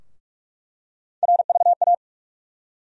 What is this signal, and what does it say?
Morse code, heard as a keyed tone, about 46 wpm, 700 Hz. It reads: RVA
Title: USA Call Sign Postfixes - 3 Letters 45wpm